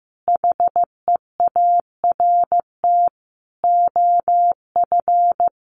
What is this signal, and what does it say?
Morse code, heard as a keyed tone, about 15 wpm, 700 Hz. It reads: HEART OF
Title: Sentences from Top 1200 Words 15wpm